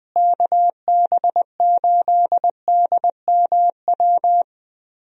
Morse code, heard as a keyed tone, about 20 wpm, 700 Hz. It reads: KB8DMW
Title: USA Call Signs - 6 Characters - Repeated 2x - 20wpm